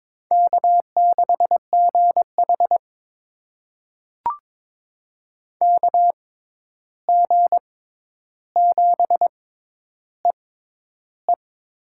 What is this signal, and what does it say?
Morse code, heard as a keyed tone, about 22 wpm, spaced as for 11 wpm, 700 Hz. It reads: K6GH KG7EE
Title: USA Call Signs - 6x Character Spacing - 22wpm